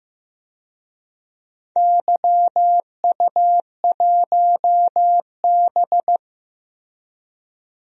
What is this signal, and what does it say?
Morse code, heard as a keyed tone, about 15 wpm, 700 Hz. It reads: YU1B
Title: DXCC-100 Call Signs - 4 Characters - Repeated 3x - 15wpm